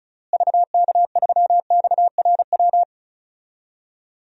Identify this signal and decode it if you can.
Morse code, heard as a keyed tone, about 34 wpm, 700 Hz. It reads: VK3XRW
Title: DXCC-100 Call Signs - 6 Characters - Repeated 2x - 35wpm